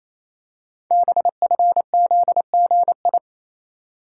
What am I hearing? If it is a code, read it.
Morse code, heard as a keyed tone, about 27 wpm, 700 Hz. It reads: BFZGI
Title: Code Groups - 5 Letters - 28wpm